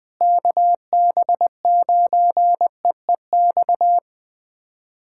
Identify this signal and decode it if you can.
Morse code, heard as a keyed tone, about 21 wpm, 700 Hz. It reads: KB9EEX
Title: USA Callsign DE Callsign - 6 Characters - 20wpm